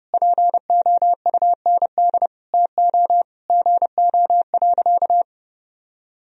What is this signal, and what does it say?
Morse code, heard as a keyed tone, about 30 wpm, 700 Hz. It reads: POUND TO GO.
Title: Sentences from Top 1000 Words - Review Part 2 - 30wpm